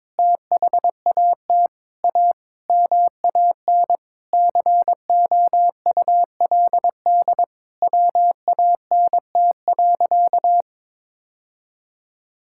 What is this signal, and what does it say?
Morse code, heard as a keyed tone, about 22 wpm, 700 Hz. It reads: THAT A MAN COULD WANT.
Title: Sentences from Top 1000 Words - Review Part 5 - 22wpm